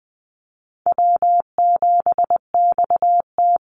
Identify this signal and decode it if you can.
Morse code, heard as a keyed tone, about 20 wpm, 700 Hz. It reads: W7XT
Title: Easy Call Signs Repeated 2x - 20wpm